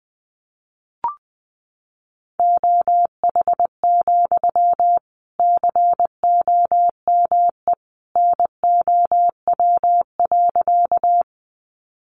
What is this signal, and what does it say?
Morse code, heard as a keyed tone, about 19 wpm, 700 Hz. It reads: OH, COME NOW.
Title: Sentences from Top 500 Words 20wpm